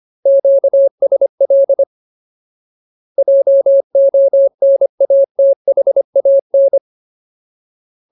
Morse code, heard as a keyed tone, about 25 wpm, 550 Hz. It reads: QSL JONATHAN
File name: Ultimate Word List / 25WPM Ultimate